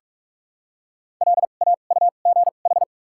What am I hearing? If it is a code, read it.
Morse code, heard as a keyed tone, about 46 wpm, spaced as for 32 wpm, 700 Hz. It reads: LAUGH
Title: 2x Character Spacing - Top 500 Words - 45wpm